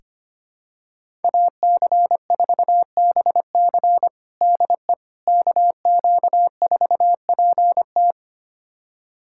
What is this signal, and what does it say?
Morse code, heard as a keyed tone, about 25 wpm, 700 Hz. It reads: AC4BC DE KQ4PT
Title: USA Callsign DE Callsign - 5 Characters - 25wpm